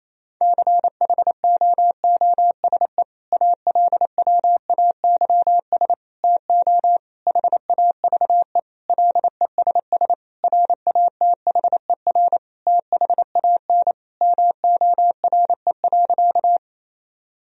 Morse code, heard as a keyed tone, about 28 wpm, 700 Hz. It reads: CHOOSE ALWAYS TO HAVE LESS RATHER THAN MORE.
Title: Sentences from Top 1000 Words - Review Part 4 - 28wpm